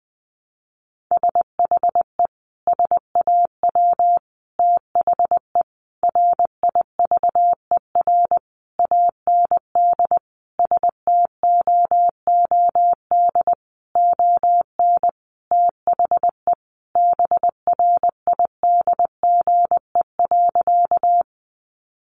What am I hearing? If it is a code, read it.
Morse code, heard as a keyed tone, about 20 wpm, 700 Hz. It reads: SHE SAW THE RIVER AND STOOD ON THE BRIDGE.
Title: Sentences from Top 1200 Words 20wpm